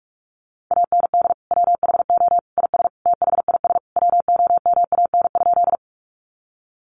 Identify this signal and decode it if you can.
Morse code, heard as a keyed tone, about 44 wpm, 700 Hz. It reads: AND WHO IS THIS WOMAN?
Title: Sentences from Top 1100 Words 45wpm